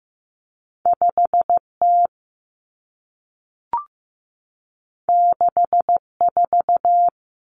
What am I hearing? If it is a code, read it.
Morse code, heard as a keyed tone, about 15 wpm, 700 Hz. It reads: 5T 64
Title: Common Cut Numbers - 2-Digit 15wpm